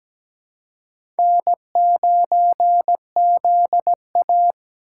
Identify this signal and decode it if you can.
Morse code, heard as a keyed tone, about 17 wpm, 700 Hz. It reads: N9ZA
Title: USA Call Signs - 4 Characters - 6x Character Spacing - 17wpm